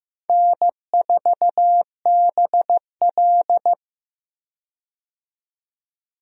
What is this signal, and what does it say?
Morse code, heard as a keyed tone, about 15 wpm, 700 Hz. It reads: N4BL
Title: USA Call Signs - 4 Characters - 15wpm